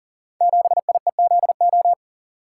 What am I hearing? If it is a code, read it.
Morse code, heard as a keyed tone, about 38 wpm, 700 Hz. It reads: 7IEZO
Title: Code Groups - 5 Letter:Numbers - 40wpm